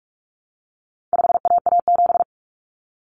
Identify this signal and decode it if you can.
Morse code, heard as a keyed tone, about 44 wpm, 700 Hz. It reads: 5AA7
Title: Common Cut Numbers - 4-Digit 45wpm